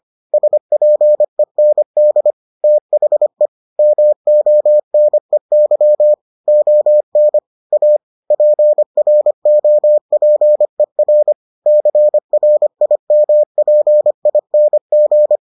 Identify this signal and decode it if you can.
Morse code, heard as a keyed tone, about 25 wpm, 600 Hz. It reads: SPEND THE MONEY ON A PROPER CRIMPING